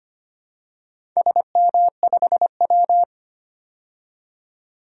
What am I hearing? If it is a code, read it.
Morse code, heard as a keyed tone, about 25 wpm, 700 Hz. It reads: SM5W
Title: DXCC-100 Call Signs - 4 Characters - 25wpm